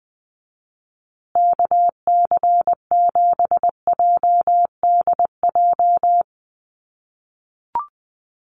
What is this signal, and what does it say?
Morse code, heard as a keyed tone, about 20 wpm, 700 Hz. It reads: KC7JDJ
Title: USA Call Signs Repeated 2x 20wpm